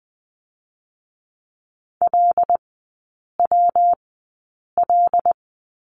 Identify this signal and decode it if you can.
Morse code, heard as a keyed tone, about 20 wpm, 700 Hz. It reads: L W L
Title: 114 Lesson 19e - Letter w - Three Characters 20wpm